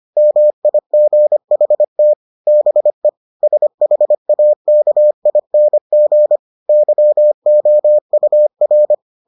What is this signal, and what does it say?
Morse code, heard as a keyed tone, about 25 wpm, 600 Hz. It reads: MIGHT BE SHAKING YOUR